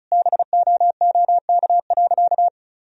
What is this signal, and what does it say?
Morse code, heard as a keyed tone, about 34 wpm, 700 Hz. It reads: BOOK.
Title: Sentences from Top 1000 Words - Review Part 1 - 35wpm